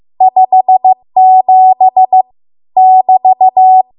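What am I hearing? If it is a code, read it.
Morse code, heard as a keyed tone, about 15 wpm, 750 Hz. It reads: 57 <BT>
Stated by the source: Morse code practice at 15 wpm from the national amateur radio association's code archive.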